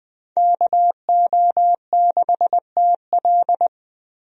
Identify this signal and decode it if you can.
Morse code, heard as a keyed tone, about 21 wpm, 700 Hz. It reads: KO6TL
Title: USA Call Signs - 5 Characters - Repeated 2x - 20wpm